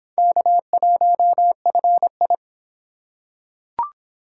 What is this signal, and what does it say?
Morse code, heard as a keyed tone, about 25 wpm, 700 Hz. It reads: K1FI